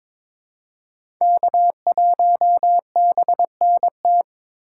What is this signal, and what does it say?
Morse code, heard as a keyed tone, about 22 wpm, 700 Hz. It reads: K1BNT
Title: USA Call Signs - 5 Characters - 3x Character Spacing - 22wpm